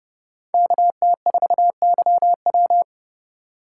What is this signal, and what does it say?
Morse code, heard as a keyed tone, about 29 wpm, 700 Hz. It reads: KT4YW